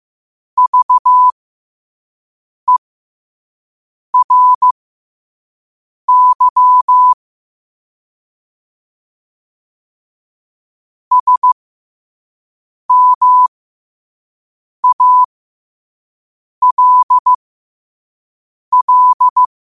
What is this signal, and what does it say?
Morse code, heard as a keyed tone, about 15 wpm, spaced as for 5 wpm, 1000 Hz. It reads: VERY SMALL
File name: 2024-10-07-5wpm